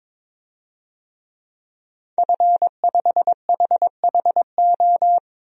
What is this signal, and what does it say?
Morse code, heard as a keyed tone, about 22 wpm, 700 Hz. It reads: F5HHO
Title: DXCC-100 Call Signs - 5 Characters - 22wpm